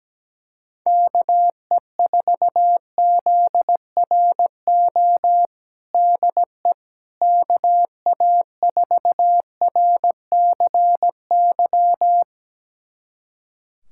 Morse code, heard as a keyed tone, about 17 wpm, 700 Hz. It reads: KE4ZRO DE KA4RCY
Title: USA Callsign DE Callsign 17wpm